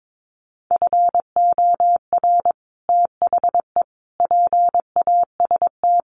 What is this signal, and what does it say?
Morse code, heard as a keyed tone, about 22 wpm, 700 Hz. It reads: FOR THE PAST
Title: Sets of 5 Words Encore2 22wpm